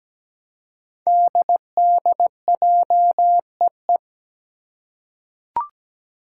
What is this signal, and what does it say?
Morse code, heard as a keyed tone, about 17 wpm, 700 Hz. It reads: DDJEE
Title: Five Challenging Letter-Number Combinations 17wpm